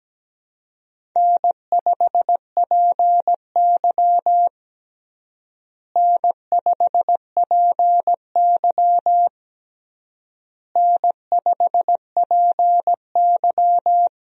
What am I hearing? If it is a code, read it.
Morse code, heard as a keyed tone, about 17 wpm, 700 Hz. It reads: N5PY N5PY N5PY
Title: USA Call Signs - 4 Characters - Repeated 3x - 17wpm